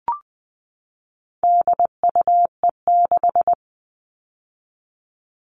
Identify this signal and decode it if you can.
Morse code, heard as a keyed tone, about 20 wpm, 700 Hz. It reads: DUE6
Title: All Cut Numbers - 4-Digit 20wpm